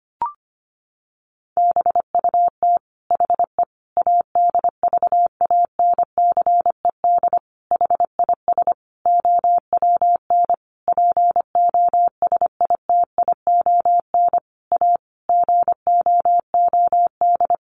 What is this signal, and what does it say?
Morse code, heard as a keyed tone, about 25 wpm, 700 Hz. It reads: BUT HE ADVANCED HIS OWN POSITION A GOOD
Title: Sentences from Top 1200 Words 25wpm